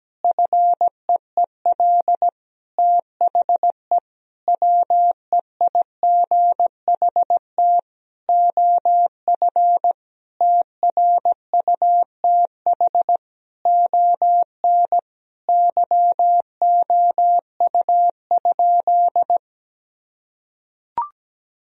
Morse code, heard as a keyed tone, about 17 wpm, 700 Hz. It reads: FEEL THE WEIGHT OF TRUTH ON YOU?
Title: Sentences from Top 1000 Words - Review Part 3 - 17wpm